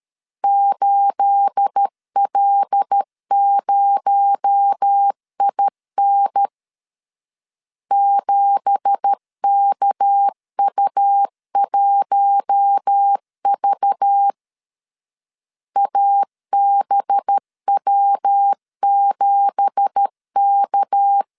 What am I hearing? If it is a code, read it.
Morse code, heard as a keyed tone, about 13 wpm, 800 Hz. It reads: 8L0IN 7KU1V ABW7K